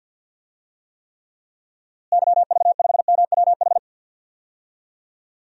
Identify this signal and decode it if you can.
Morse code, heard as a keyed tone, about 52 wpm, 700 Hz. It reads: YV5MWH
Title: DXCC-100 Call Signs - 6 Characters - Repeated 2x - 50wpm